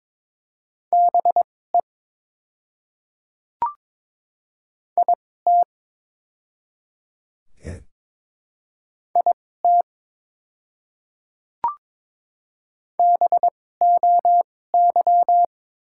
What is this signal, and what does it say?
Morse code, heard as a keyed tone, about 21 wpm, spaced as for 16 wpm, 700 Hz. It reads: BE IT IT BOY
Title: 2x Character Spacing - Top 300 Words - 22wpm